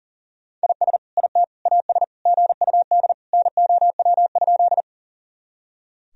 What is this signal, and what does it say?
Morse code, heard as a keyed tone, about 40 wpm, 700 Hz. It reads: IS IT AS GUD NOW?
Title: Sentences from Top 200 Words and Abbreviations 40wpm